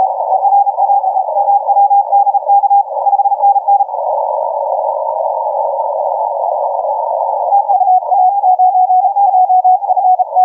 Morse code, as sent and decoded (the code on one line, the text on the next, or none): none